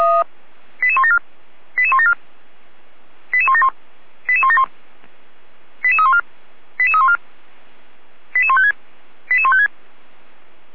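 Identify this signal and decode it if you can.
Five-tone selective call (ZVEI), heard as quick runs of five tones, tones of about 70 ms. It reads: 88174 88174 88171 88171 88314 88314 88156 88156
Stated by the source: Five-tone ZVEI selective call received by SDR, from audio server of Feuerwehr Orlamünde (fire brigade), code 88307.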